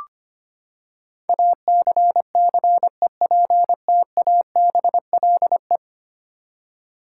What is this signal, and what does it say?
Morse code, heard as a keyed tone, about 25 wpm, 700 Hz. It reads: ACCEPTABLE
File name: Ten Letter Words - 25wpm